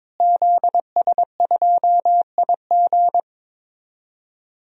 Morse code, heard as a keyed tone, about 21 wpm, 700 Hz. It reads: ZS2IG
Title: DXCC-100 Call Signs - 5 Characters - Repeated 3x - 22wpm